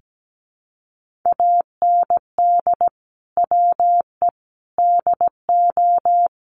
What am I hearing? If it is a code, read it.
Morse code, heard as a keyed tone, about 17 wpm, 700 Hz. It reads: AND WE DO
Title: Sets of 5 Words Encore2 17wpm